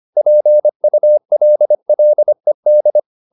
Morse code, heard as a keyed tone, about 25 wpm, 600 Hz. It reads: PULLED
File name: the old man and the sea part 1 25wpm